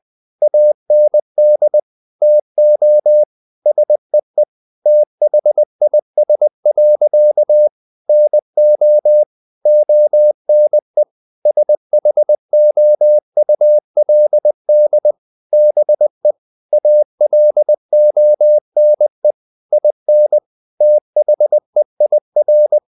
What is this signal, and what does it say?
Morse code, heard as a keyed tone, about 20 wpm, 600 Hz. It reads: AND TO SEE THIS. NO ONE SHOULD BE ALONE IN THEIR
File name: the old man and the sea / the old man and the sea part 1 20wpm